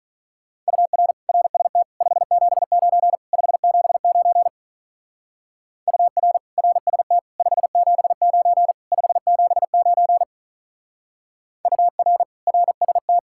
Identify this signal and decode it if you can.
Morse code, heard as a keyed tone, about 44 wpm, 700 Hz. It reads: UR RST 579 579 UR RST 579 579 UR RST
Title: Speed-Racing Common QSO Phrases - 35wpm